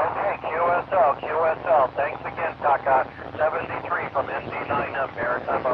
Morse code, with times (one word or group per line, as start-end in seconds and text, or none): none